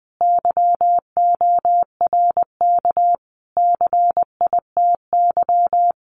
Morse code, heard as a keyed tone, about 20 wpm, 700 Hz. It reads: YORK CITY
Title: Sets of 5 Words Encore2 20wpm